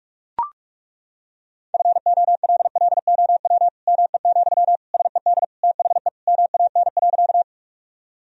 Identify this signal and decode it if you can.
Morse code, heard as a keyed tone, about 42 wpm, 700 Hz. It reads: FOLLOW ME, SED THE MAN.
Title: Sentences from Top 200 Words and Abbreviations 45wpm